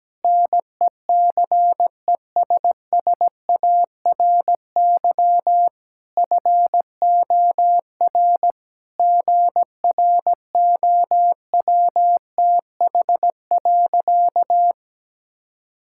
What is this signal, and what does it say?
Morse code, heard as a keyed tone, about 17 wpm, 700 Hz. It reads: NECESSARY FOR GROWTH.
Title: Sentences from Top 1000 Words - Review Part 5 - 17wpm